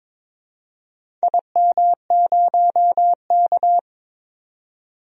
Morse code, heard as a keyed tone, about 22 wpm, 700 Hz. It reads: IM0K
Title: DXCC-100 Call Signs - 4 Characters - 22wpm